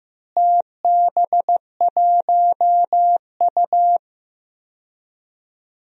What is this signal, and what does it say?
Morse code, heard as a keyed tone, about 15 wpm, 700 Hz. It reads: TB1U
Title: DXCC-100 Call Signs - 4 Characters - Repeated 2x - 15wpm